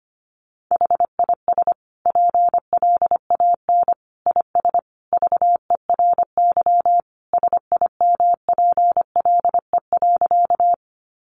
Morse code, heard as a keyed tone, about 25 wpm, 700 Hz. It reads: HIS PLAN IS VERY SIMPLE.
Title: Sentences from Top 500 Words 25wpm